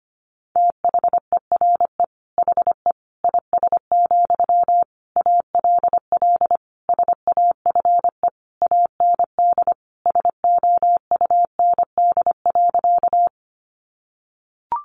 Morse code, heard as a keyed tone, about 25 wpm, 700 Hz. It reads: THERE HE IS, ALL SAFE AND SOUND.
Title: Sentences from Top 900 Words 25wpm